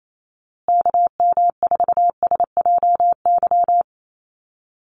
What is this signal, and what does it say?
Morse code, heard as a keyed tone, about 28 wpm, 700 Hz. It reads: KM4SJY
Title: USA Call Signs Repeated 2x 28wpm